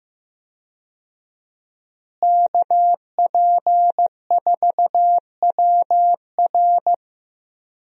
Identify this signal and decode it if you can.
Morse code, heard as a keyed tone, about 15 wpm, 700 Hz. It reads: KP4WR
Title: USA Call Signs - 5 Characters - Repeated 2x - 15wpm